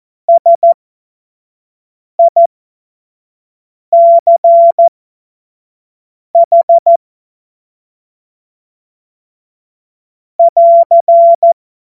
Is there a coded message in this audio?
Yes: Morse code from 0.3 to 11.5 s, heard as a keyed tone, about 14 wpm, spaced as for 5 wpm, 680 Hz. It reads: SICH <AR>